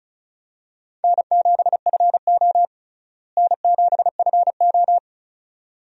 Morse code, heard as a keyed tone, about 36 wpm, 700 Hz. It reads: N7FO N7FO
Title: USA Call Signs - 4 Characters - Repeated 2x - 35wpm